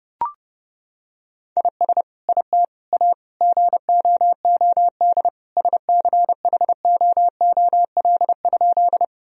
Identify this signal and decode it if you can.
Morse code, heard as a keyed tone, about 30 wpm, 700 Hz. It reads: IS IT A GOOD SCHOOL?
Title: Sentences from Top 1000 Words - Review Part 1 - 30wpm